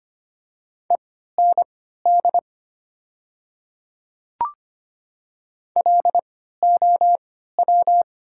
Morse code, heard as a keyed tone, about 25 wpm, spaced as for 14 wpm, 700 Hz. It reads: END LOW
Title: 3x Character Spacing - Top 200 Words - 25wpm